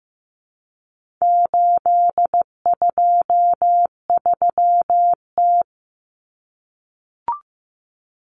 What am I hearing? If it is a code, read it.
Morse code, heard as a keyed tone, about 15 wpm, 700 Hz. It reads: 823T
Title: Common Cut Numbers - 4-Digit 15wpm